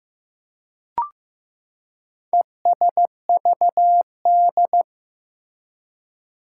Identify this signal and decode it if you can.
Morse code, heard as a keyed tone, about 15 wpm, 700 Hz. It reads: ESVD
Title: Four Challenging Letter-Number Combinations 15wpm